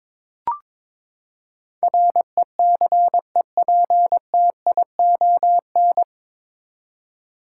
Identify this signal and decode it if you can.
Morse code, heard as a keyed tone, about 22 wpm, 700 Hz. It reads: RECEPTION
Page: Morse Code Ninja practice 22wpm